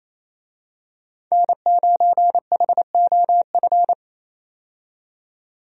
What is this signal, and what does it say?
Morse code, heard as a keyed tone, about 28 wpm, 700 Hz. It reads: N9HOF
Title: USA Call Signs - 5 Characters - 28wpm